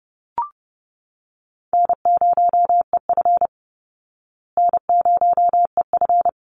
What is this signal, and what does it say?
Morse code, heard as a keyed tone, about 30 wpm, 700 Hz. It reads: N0EF N0EF
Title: Easy Call Signs Repeated 2x - 30wpm